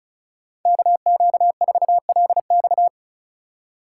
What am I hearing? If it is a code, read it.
Morse code, heard as a keyed tone, about 36 wpm, 700 Hz. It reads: KQ4LX
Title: USA Call Signs - 5 Characters - 3x Character Spacing - 35wpm